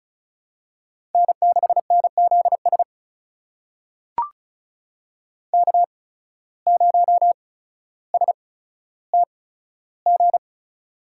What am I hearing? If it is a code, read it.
Morse code, heard as a keyed tone, about 36 wpm, spaced as for 15 wpm, 700 Hz. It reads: N6NZS K0STG
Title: USA Call Signs - 5 Characters - 8x Character Spacing - 35wpm